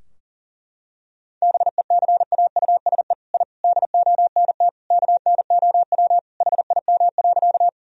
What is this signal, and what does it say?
Morse code, heard as a keyed tone, about 40 wpm, 700 Hz. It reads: BECAUSE I DONT KNOW HIM.
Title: Sentences from Top 1000 Words - Review Part 3 - 40wpm